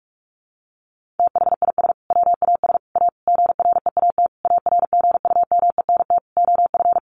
Morse code, heard as a keyed tone, about 44 wpm, 700 Hz. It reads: THIS WAS A GREAT ARGUMENT OF
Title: Sentences from Top 1400 Words 45wpm